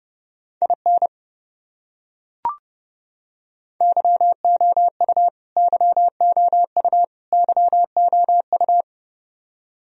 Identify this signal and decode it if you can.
Morse code, heard as a keyed tone, about 30 wpm, 700 Hz. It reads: IN YOU YOU YOU
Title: Top 100 Words Repeated 30wpm